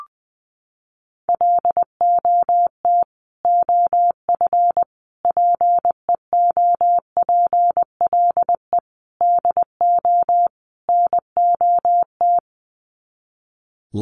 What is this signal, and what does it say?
Morse code, heard as a keyed tone, about 20 wpm, 700 Hz. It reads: LOT OF PEOPLE DO NOT
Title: Sets of 5 Words Encore 20wpm